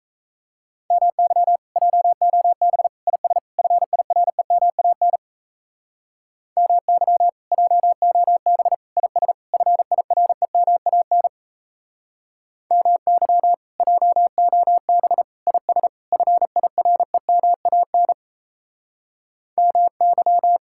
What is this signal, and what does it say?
Morse code, heard as a keyed tone, about 37 wpm, 700 Hz. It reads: MY JOB IS FIREMAN MY JOB IS FIREMAN MY JOB IS FIREMAN MY